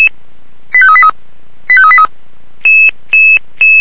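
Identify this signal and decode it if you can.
Five-tone selective call (ZVEI), heard as quick runs of five tones, tones of about 70 ms. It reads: 85383 85383
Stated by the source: ZVEI five-tone selective call received on an SDR.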